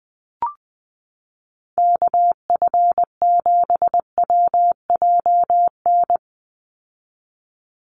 Morse code, heard as a keyed tone, about 20 wpm, 700 Hz. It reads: KF7WJN
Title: USA Call Signs 20wpm